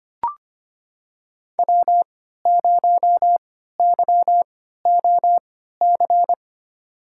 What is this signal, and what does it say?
Morse code, heard as a keyed tone, about 25 wpm, spaced as for 14 wpm, 700 Hz. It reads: W0YOC
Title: USA Call Signs - 5 Characters - 3x Character Spacing - 25wpm